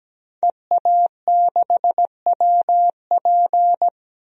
Morse code, heard as a keyed tone, about 17 wpm, 700 Hz. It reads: EA6WP
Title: DXCC-100 Call Signs - 5 Characters - Repeated 3x - 17wpm